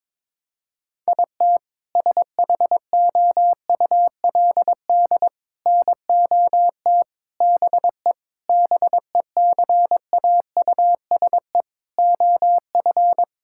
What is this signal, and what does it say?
Morse code, heard as a keyed tone, about 22 wpm, 700 Hz. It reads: IT SHOULD NOT BE BECAUSE OF
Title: Sentences from Top 1600 Words 22wpm